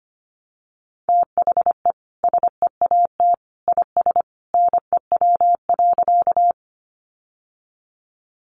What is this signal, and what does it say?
Morse code, heard as a keyed tone, about 25 wpm, 700 Hz. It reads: THE SEAT IS NEW.
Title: Sentences from Top 1500 Words 25wpm